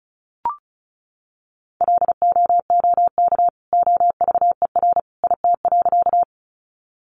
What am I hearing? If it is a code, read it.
Morse code, heard as a keyed tone, about 34 wpm, 700 Hz. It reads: LOOK OVER IT.
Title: Sentences from Top 100 Words 35wpm